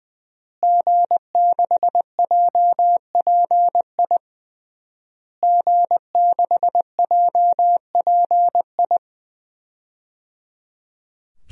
Morse code, heard as a keyed tone, about 21 wpm, 700 Hz. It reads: G6JPI G6JPI
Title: DXCC-100 Call Signs - 5 Characters - Repeated 3x - 20wpm